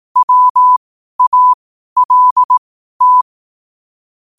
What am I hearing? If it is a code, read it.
Morse code, heard as a keyed tone, about 18 wpm, spaced as for 12 wpm, 1000 Hz. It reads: WALT
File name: qotdfiles / 12wpm